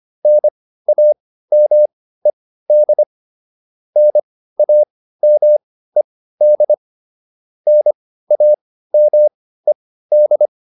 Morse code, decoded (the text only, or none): NAMED NAMED NAMED